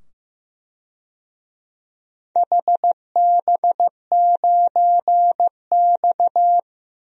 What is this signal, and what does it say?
Morse code, heard as a keyed tone, about 15 wpm, 700 Hz. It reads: HB9X
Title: DXCC-100 Call Signs - 4 Characters - 15wpm